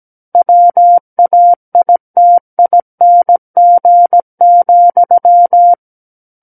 Morse code, heard as a keyed tone, about 17 wpm, 700 Hz. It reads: WAITING,
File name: Book One / 08-Chapter 8 - 17 WPM